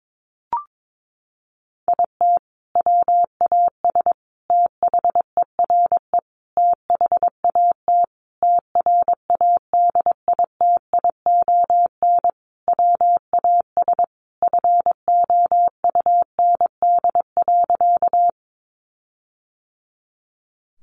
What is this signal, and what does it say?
Morse code, heard as a keyed tone, about 22 wpm, 700 Hz. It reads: IT WAS THERE THAT TRADITION WAS FOUND.
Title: Sentences from Top 1500 Words 22wpm